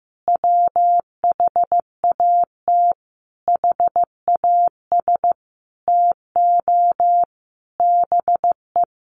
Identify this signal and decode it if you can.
Morse code, heard as a keyed tone, about 15 wpm, 700 Hz. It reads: WHAT HAS TO BE
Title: Sentences from Top 700 Words 15wpm